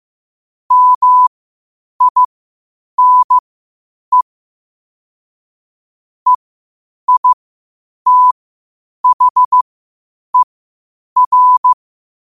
Morse code, heard as a keyed tone, about 15 wpm, spaced as for 8 wpm, 1000 Hz. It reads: MINE EITHER
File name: qotdfiles / 8wpm